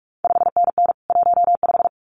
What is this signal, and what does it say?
Morse code, heard as a keyed tone, about 46 wpm, 700 Hz. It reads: 5NN 15
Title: CQ WW DX Contest Exchange - 45wpm